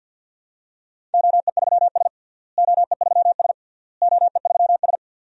Morse code, heard as a keyed tone, about 52 wpm, 700 Hz. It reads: OE3S OE3S OE3S
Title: DXCC-100 Call Signs - 4 Characters - Repeated 3x - 50wpm